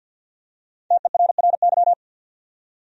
Morse code, heard as a keyed tone, about 48 wpm, 700 Hz. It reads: TERRY
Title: USA Common Names - 50wpm